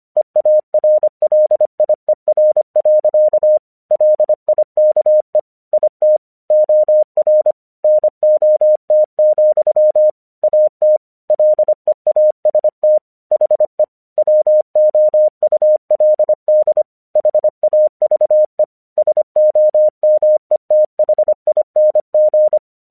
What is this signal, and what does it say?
Morse code, heard as a keyed tone, about 25 wpm, 610 Hz. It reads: EARLIER. LIKE IT OR NOT, AT LEAST HE WOULD HAVE SOMETHING
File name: 1 English 25wpm / AQC17